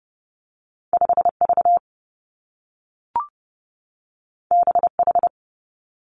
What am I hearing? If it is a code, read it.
Morse code, heard as a keyed tone, about 30 wpm, 700 Hz. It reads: 5 V B H
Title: Two Challenging Letter-Number Combinations 30wpm